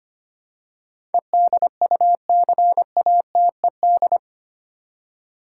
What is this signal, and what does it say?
Morse code, heard as a keyed tone, about 25 wpm, 700 Hz. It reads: EDUCATED
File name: Eight Letter Words - 25wpm